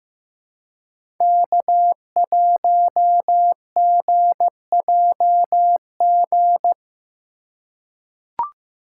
Morse code, heard as a keyed tone, about 15 wpm, 700 Hz. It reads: K1GJG
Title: USA Call Signs - 5 Characters - 3x Character Spacing - 15wpm